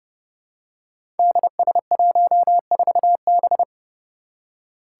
Morse code, heard as a keyed tone, about 30 wpm, 700 Hz. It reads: DS14B
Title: Five Challenging Letter-Number Combinations 30wpm